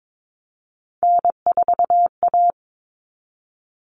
Morse code, heard as a keyed tone, about 22 wpm, 700 Hz. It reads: N4A
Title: All Cut Numbers - 3-Digit 22wpm